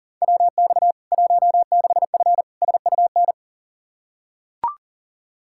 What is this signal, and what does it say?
Morse code, heard as a keyed tone, about 37 wpm, 700 Hz. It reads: WX 16F SUN
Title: K1USN SST QSO Practice 40wpm